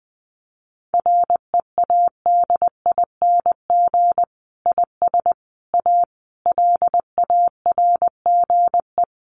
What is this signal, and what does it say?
Morse code, heard as a keyed tone, about 20 wpm, 700 Hz. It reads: READING IS A LARGE
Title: Sentences from Top 1400 Words 20wpm